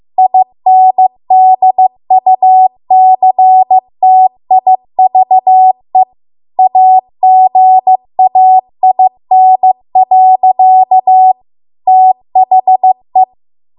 Morse code, heard as a keyed tone, about 15 wpm, 750 Hz. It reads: INDUCTIVE AGAIN. THE